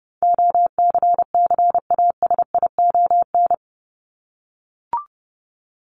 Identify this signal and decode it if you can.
Morse code, heard as a keyed tone, about 30 wpm, 700 Hz. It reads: OCCASION
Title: Top 1501-1750 Words 30wpm